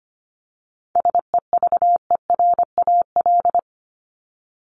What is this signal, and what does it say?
Morse code, heard as a keyed tone, about 25 wpm, 700 Hz. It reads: SEVERAL
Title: Top 500 Words 25wpm